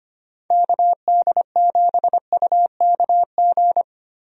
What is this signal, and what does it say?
Morse code, heard as a keyed tone, about 25 wpm, 700 Hz. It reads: KD7UKG
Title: USA Call Signs - 6 Characters - Repeated 2x - 25wpm